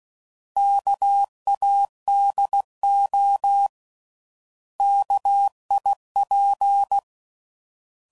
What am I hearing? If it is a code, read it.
Morse code, heard as a keyed tone, about 16 wpm, 800 Hz. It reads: KADO KIP
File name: Prefixes / By First Letter / Ks